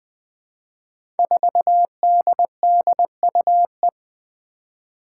Morse code, heard as a keyed tone, about 21 wpm, 700 Hz. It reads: 4DDUE
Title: Five Challenging Letter-Number Combinations 20wpm